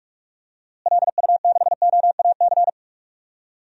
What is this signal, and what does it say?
Morse code, heard as a keyed tone, about 42 wpm, 700 Hz. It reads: LU6OAC
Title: DXCC-100 Call Signs - 6 Characters - Repeated 2x - 45wpm